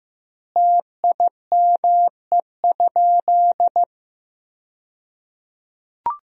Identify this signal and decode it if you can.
Morse code, heard as a keyed tone, about 16 wpm, 700 Hz. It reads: TIME?
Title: 2x Word-Spacing - Sentences from Top 100 Words - 15wpm